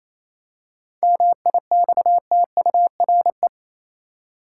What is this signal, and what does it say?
Morse code, heard as a keyed tone, about 28 wpm, 700 Hz. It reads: MIXTURE
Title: Seven Letter Words - 28wpm